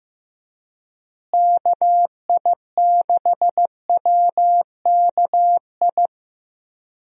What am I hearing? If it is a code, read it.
Morse code, heard as a keyed tone, about 15 wpm, 700 Hz. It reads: KI6WKI
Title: USA Call Signs - 6 Characters - Repeated 3x - 15wpm